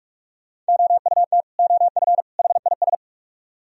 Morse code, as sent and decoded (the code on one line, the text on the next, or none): --- ..- - / --- ..-. / .... .. ...
OUT OF HIS